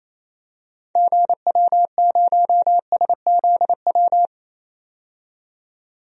Morse code, heard as a keyed tone, about 28 wpm, 700 Hz. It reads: GW0SZW
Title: DXCC-100 Call Signs - 6 Characters - 28wpm